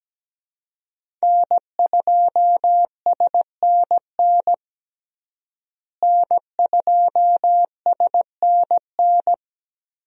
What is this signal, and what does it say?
Morse code, heard as a keyed tone, about 17 wpm, 700 Hz. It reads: N2SNN N2SNN
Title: USA Call Signs - 5 Characters - Repeated 3x - 17wpm